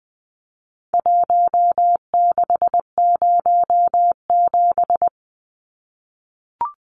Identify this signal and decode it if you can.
Morse code, heard as a keyed tone, about 20 wpm, 700 Hz. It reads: 1607
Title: Numbers - Contest Style - 4-Digit 20wpm